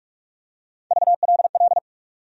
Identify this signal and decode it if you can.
Morse code, heard as a keyed tone, about 42 wpm, 700 Hz. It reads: VLL
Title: USA Call Sign Postfixes - 3 Letters 45wpm